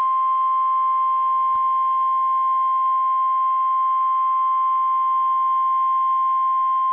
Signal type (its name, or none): Emergency Alert System (SAME) tones